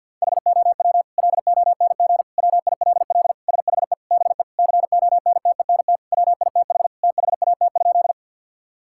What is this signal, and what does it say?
Morse code, heard as a keyed tone, about 50 wpm, 700 Hz. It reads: HOW LONG WILL SHE BE CONTENT WITH THAT?
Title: Sentences from Top 1000 Words - Review Part 6 - 50wpm